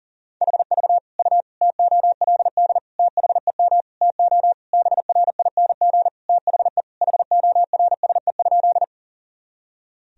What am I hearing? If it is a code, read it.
Morse code, heard as a keyed tone, about 40 wpm, 700 Hz. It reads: HV U TOLD THEM TO BRING THE HORSE?
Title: Sentences from Top 500 Words and Abbreviations 40wpm